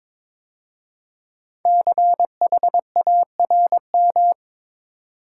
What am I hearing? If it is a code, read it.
Morse code, heard as a keyed tone, about 23 wpm, 700 Hz. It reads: CHARM
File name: 5 Letter Words - First 3 Letters Given - 22wpm